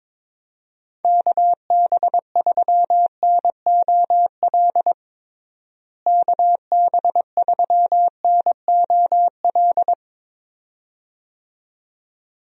Morse code, heard as a keyed tone, about 22 wpm, 700 Hz. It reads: KB3NOL KB3NOL
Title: USA Call Signs - 6 Characters - Repeated 2x - 22wpm